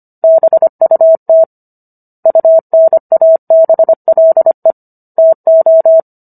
Morse code, heard as a keyed tone, about 25 wpm, 650 Hz. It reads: BUT UNABLE TO
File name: Book One / 16-Chapter 16 - 25 WPM